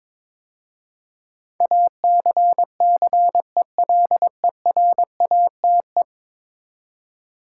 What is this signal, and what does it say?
Morse code, heard as a keyed tone, about 22 wpm, 700 Hz. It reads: ACCELERATE
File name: practice / 22wpm / 10 Letter Words - First 3 Letters Given - 22wpm